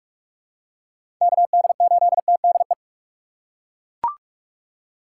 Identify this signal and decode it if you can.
Morse code, heard as a keyed tone, about 42 wpm, 700 Hz. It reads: K D 8 T D E
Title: USA Call Signs - 6 Characters - 6x Character Spacing - 45wpm